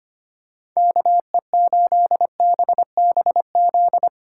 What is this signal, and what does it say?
Morse code, heard as a keyed tone, about 25 wpm, 700 Hz. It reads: KE8BBZ
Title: USA Callsign DE Callsign - 6 Characters - 25wpm